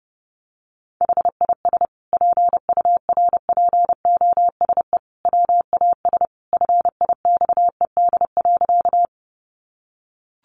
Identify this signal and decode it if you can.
Morse code, heard as a keyed tone, about 30 wpm, 700 Hz. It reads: HIS PURPOSE WAS FIXED.
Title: Sentences from Top 1100 Words 30wpm